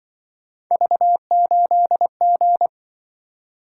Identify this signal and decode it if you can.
Morse code, heard as a keyed tone, about 24 wpm, 700 Hz. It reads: V8G
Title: Speed-Racing - Three Letter-Number Combinations - 20wpm